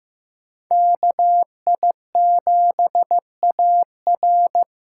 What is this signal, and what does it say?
Morse code, heard as a keyed tone, about 15 wpm, 700 Hz. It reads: KI7AR
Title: USA Call Signs - 8x Character Spacing - 15wpm